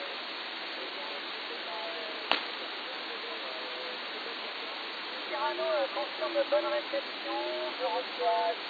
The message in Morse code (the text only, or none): none